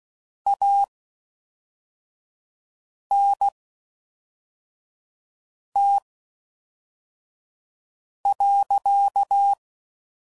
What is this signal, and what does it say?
Morse code, heard as a keyed tone, about 16 wpm, spaced as for 4 wpm, 800 Hz. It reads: ANT.